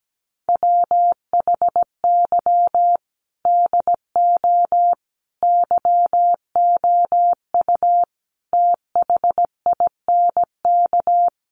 Morse code, heard as a keyed tone, about 17 wpm, 700 Hz. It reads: WHY DO YOU THINK
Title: Sets of 5 Words Encore2 17wpm